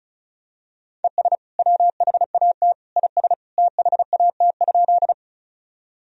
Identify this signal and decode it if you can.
Morse code, heard as a keyed tone, about 36 wpm, 700 Hz. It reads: ES WHAT IS THAT?
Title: Sentences from Top 100 Words and Abbreviations 35wpm